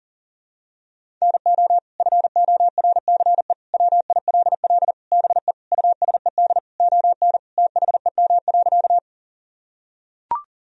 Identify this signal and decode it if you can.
Morse code, heard as a keyed tone, about 40 wpm, 700 Hz. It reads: NO FORCE WILL BE USED ON THEM.
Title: Sentences from Top 1000 Words - Review Part 3 - 40wpm